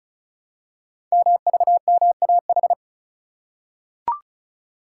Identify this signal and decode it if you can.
Morse code, heard as a keyed tone, about 36 wpm, 690 Hz. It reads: MVMAH
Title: Code Groups - 5 Letters - 35wpm